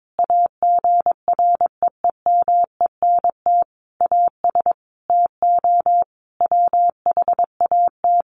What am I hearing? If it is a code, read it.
Morse code, heard as a keyed tone, about 22 wpm, 700 Hz. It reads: AGREEMENT AS TO WHAT
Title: Sentences from Top 1500 Words 22wpm